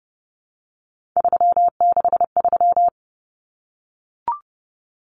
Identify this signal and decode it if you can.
Morse code, heard as a keyed tone, about 30 wpm, 700 Hz. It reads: 363
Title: Common Cut Numbers - 3-Digit 30wpm